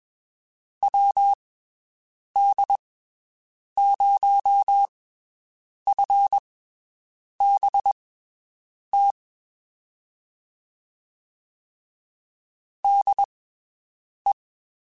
Morse code, heard as a keyed tone, about 22 wpm, spaced as for 7 wpm, 800 Hz. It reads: WD0FBT DE
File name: (spaced) / QSOs Spaced-mp3s / WD0FBT 1 @20 sp